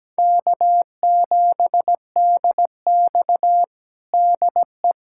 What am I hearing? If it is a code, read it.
Morse code, heard as a keyed tone, about 17 wpm, 700 Hz. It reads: K7DX DE
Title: USA Callsign DE Callsign - 4 Characters - 17wpm